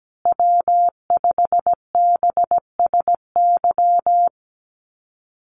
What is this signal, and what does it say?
Morse code, heard as a keyed tone, about 17 wpm, 700 Hz. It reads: W5BSY
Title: USA Call Signs Repeated 2x 17wpm